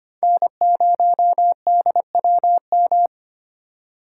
Morse code, heard as a keyed tone, about 25 wpm, 700 Hz. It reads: N0DWM
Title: USA Call Signs - 5 Characters - 4x Character Spacing - 25wpm